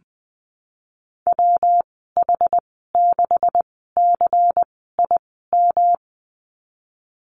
Morse code, heard as a keyed tone, about 20 wpm, spaced as for 14 wpm, 700 Hz. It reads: WH6CIM
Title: 179 Lesson 29g - Slash - Call Signs 20wpm